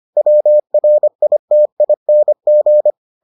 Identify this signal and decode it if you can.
Morse code, heard as a keyed tone, about 25 wpm, 600 Hz. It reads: WRITING